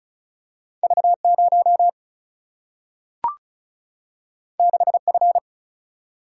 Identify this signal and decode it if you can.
Morse code, heard as a keyed tone, about 36 wpm, 700 Hz. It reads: V 0 6 F
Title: Single-Letter Single-Number 35wpm